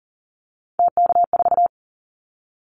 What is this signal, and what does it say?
Morse code, heard as a keyed tone, about 40 wpm, 700 Hz. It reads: TK4
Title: Callsign Prefixes for Top 100 DXCC - 40wpm